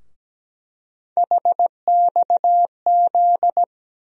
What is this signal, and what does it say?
Morse code, heard as a keyed tone, about 17 wpm, 700 Hz. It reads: HXZ
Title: USA Call Sign Postfixes - 3 Letters 17wpm